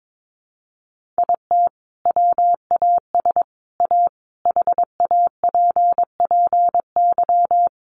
Morse code, heard as a keyed tone, about 22 wpm, 700 Hz. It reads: IT WAS A HAPPY
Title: Sentences from Top 1000 Words 22wpm